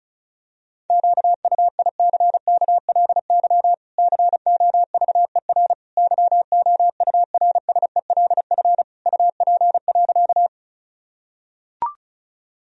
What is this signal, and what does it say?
Morse code, heard as a keyed tone, about 36 wpm, 700 Hz. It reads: QUICKLY COVER YOURSELF UP.
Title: Sentences from Top 1000 Words - Review Part 1 - 35wpm